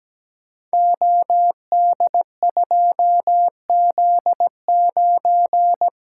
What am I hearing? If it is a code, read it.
Morse code, heard as a keyed tone, about 17 wpm, 700 Hz. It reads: OD2Z9
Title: Code Groups - 5 Letter:Numbers - 17wpm